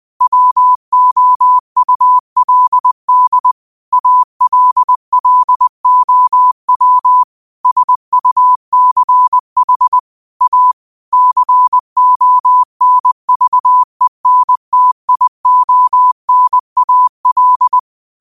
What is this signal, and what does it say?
Morse code, heard as a keyed tone, about 20 wpm, 1000 Hz. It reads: WOULD ALLOW SUCH A CONVENTIONAL
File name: 20wpm